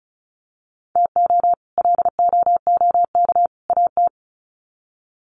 Morse code, heard as a keyed tone, about 34 wpm, 700 Hz. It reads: TO LOOK AT
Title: Sets of 3 Words 35wpm